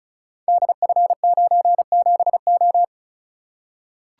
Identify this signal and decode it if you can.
Morse code, heard as a keyed tone, about 36 wpm, 700 Hz. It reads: DF97O
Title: Code Groups - 5 Letter:Numbers - 35wpm